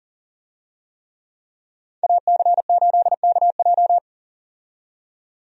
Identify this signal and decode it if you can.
Morse code, heard as a keyed tone, about 40 wpm, 700 Hz. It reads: AC8KJ
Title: USA Call Signs - 5 Characters - 40wpm